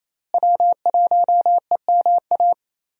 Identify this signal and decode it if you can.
Morse code, heard as a keyed tone, about 28 wpm, 700 Hz. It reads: W1EMA
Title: USA Call Signs - 5 Characters - Repeated 3x - 28wpm